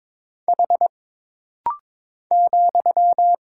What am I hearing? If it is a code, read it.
Morse code, heard as a keyed tone, about 22 wpm, 700 Hz. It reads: H ,
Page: This rapid-fire timing is geared to push your ICR (Instant Character Recognition) skills.